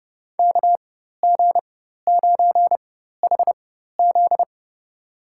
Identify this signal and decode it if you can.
Morse code, heard as a keyed tone, about 32 wpm, spaced as for 14 wpm, 700 Hz. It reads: KG9HZ
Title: USA Call Signs - 5 Characters - 4x Character Spacing - 30wpm